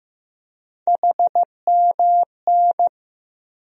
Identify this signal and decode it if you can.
Morse code, heard as a keyed tone, about 15 wpm, 700 Hz. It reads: HMN
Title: USA Call Sign Postfixes - 3 Letters 15wpm